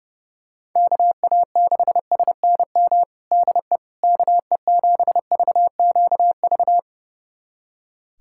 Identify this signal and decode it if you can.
Morse code, heard as a keyed tone, about 30 wpm, 700 Hz. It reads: KA6SNM DE KE7VQV
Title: USA Callsign DE Callsign - 6 Characters - 30wpm